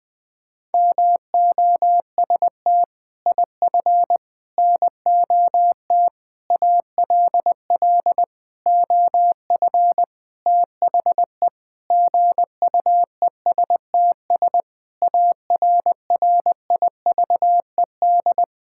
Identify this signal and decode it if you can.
Morse code, heard as a keyed tone, about 20 wpm, 700 Hz. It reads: MOST IF NOT ALL OF THE GUESTS ARRIVED
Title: Sentences with 6-Word Phrases - 20wpm